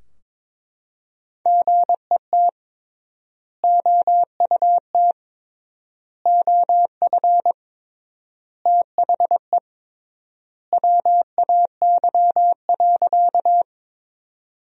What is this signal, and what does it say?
Morse code, heard as a keyed tone, about 22 wpm, 700 Hz. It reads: GET OUT OF THE WAY.
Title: 3x Word-Spacing - Sentences from Top 200 Words - 22wpm